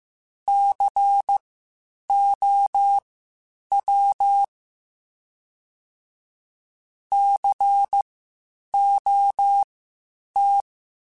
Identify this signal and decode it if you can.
Morse code, heard as a keyed tone, about 15 wpm, spaced as for 8 wpm, 800 Hz. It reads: COW COT
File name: Spaced Alphas / Spaced Alphas mp3s / Cs sp @13